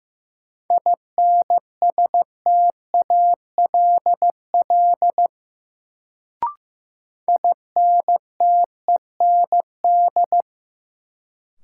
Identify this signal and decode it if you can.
Morse code, heard as a keyed tone, about 15 wpm, 700 Hz. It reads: INSTALL INTEND
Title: Sentences from Top 1600 Words - New Word Drill Alt - 15wpm